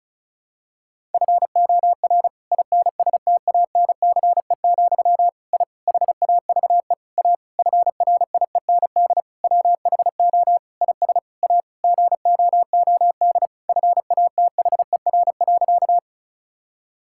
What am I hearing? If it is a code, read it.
Morse code, heard as a keyed tone, about 36 wpm, 700 Hz. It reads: FOR INSTANCE, I HAVE A FRIEND WHO IS A GOOD FATHER.
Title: Sentences from Top 1000 Words - Review Part 5 - 35wpm